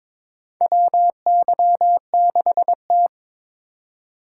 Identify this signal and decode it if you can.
Morse code, heard as a keyed tone, about 21 wpm, 700 Hz. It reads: WY6T
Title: USA Call Signs - 4 Characters - 3x Character Spacing - 22wpm